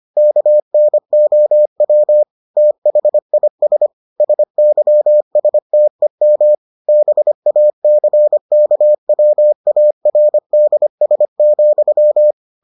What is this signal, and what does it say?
Morse code, heard as a keyed tone, about 25 wpm, 600 Hz. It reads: KNOW THIS SYSTEM BACKWARDS,